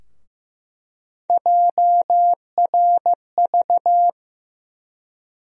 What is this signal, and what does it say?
Morse code, heard as a keyed tone, about 15 wpm, 700 Hz. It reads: JRV
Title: USA Call Sign Postfixes - 3 Letters 15wpm